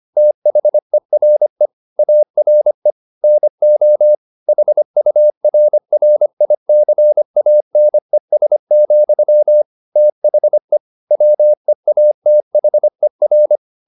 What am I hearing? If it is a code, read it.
Morse code, heard as a keyed tone, about 25 wpm, 600 Hz. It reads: THERE ARE NO HURRICANES, THE WEATHER
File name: the old man and the sea part 1 25wpm